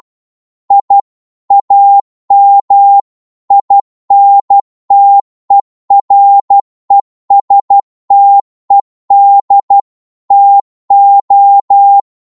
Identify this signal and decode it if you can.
Morse code, heard as a keyed tone, about 12 wpm, 800 Hz. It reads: I AM INTERESTED TO
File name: Morse / qso-12wpm